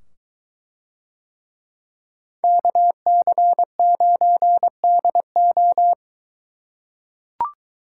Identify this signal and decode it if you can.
Morse code, heard as a keyed tone, about 23 wpm, 700 Hz. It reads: KC9DO